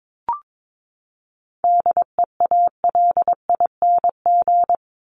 Morse code, heard as a keyed tone, about 22 wpm, 700 Hz. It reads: DEALING
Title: ING Suffix - Top 100 Words with Base Words 22wpm